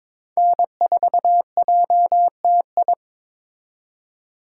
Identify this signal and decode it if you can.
Morse code, heard as a keyed tone, about 26 wpm, 700 Hz. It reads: N4JTI
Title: Speed Racing - USA 1x3 Call Signs 22wpm